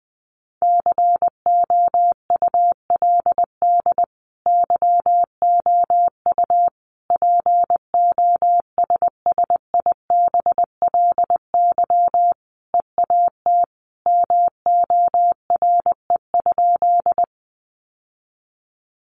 Morse code, heard as a keyed tone, about 20 wpm, 700 Hz. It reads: COULD YOU POSSIBLY EAT MORE?
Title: Sentences from Top 1300 Words 20wpm